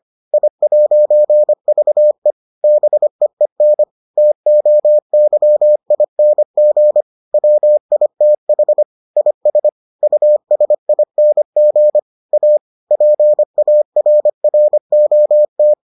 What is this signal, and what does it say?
Morse code, heard as a keyed tone, about 25 wpm, 600 Hz. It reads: I'VE BEEN TOYING WITH IS USING A PARROT